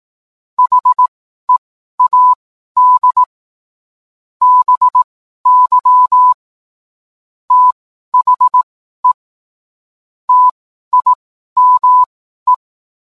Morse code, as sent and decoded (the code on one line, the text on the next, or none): .... .- -.. / -... -.-- / - .... / - .. -- .
HEAD BY THE TIME